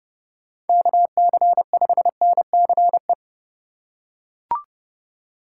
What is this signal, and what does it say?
Morse code, heard as a keyed tone, about 30 wpm, 700 Hz. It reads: KC5NCE
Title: USA Call Signs - 6 Characters - 6x Character Spacing - 30wpm